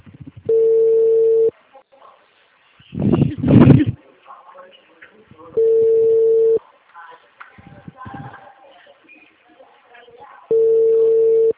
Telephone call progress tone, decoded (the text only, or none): ringback tone